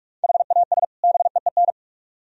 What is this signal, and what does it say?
Morse code, heard as a keyed tone, about 45 wpm, 700 Hz. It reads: HAS BEEN